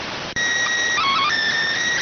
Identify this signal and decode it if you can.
SSTV (slow-scan television), PD 290.